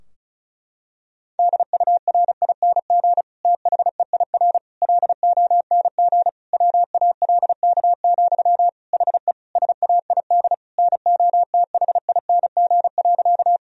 Morse code, heard as a keyed tone, about 34 wpm, 700 Hz. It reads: DURING THEIR LONG WALK, HE SAID NOTHING.
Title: Sentences from Top 1000 Words - Review Part 2 - 35wpm